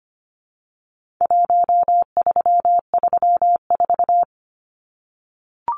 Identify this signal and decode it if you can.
Morse code, heard as a keyed tone, about 25 wpm, 700 Hz. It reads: 1334
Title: Numbers - Contest Style - 4-Digit 25wpm